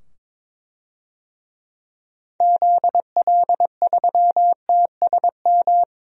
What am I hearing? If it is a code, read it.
Morse code, heard as a keyed tone, about 21 wpm, 700 Hz. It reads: ZL3TSM
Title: DXCC-100 Call Signs - 6 Characters - 22wpm